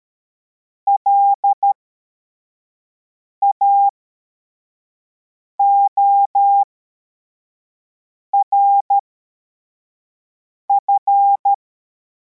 Morse code, heard as a keyed tone, about 13 wpm, spaced as for 4 wpm, 800 Hz. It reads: LAORF
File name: formation / cw / Text81